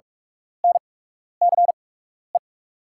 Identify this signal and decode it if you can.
Morse code, heard as a keyed tone, about 44 wpm, spaced as for 12 wpm, 690 Hz. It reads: NCE
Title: 8x Character Spacing - Top 500 Words - 45wpm